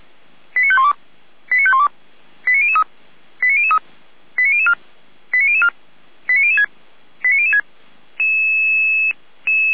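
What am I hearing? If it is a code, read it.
Five-tone selective call (ZVEI), heard as quick runs of five tones, tones of about 80 ms. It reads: 89512 89512 89003 89003 89004 89004 89006 89006
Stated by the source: ZVEI five-tone call received by SDR, server of feuerwehr-orlamuende.org (fire brigade Orlamünde).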